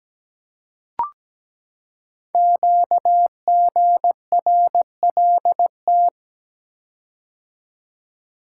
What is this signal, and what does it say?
Morse code, heard as a keyed tone, about 17 wpm, 700 Hz. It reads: QGRLT
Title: Code Groups - 5 Letters - 17wpm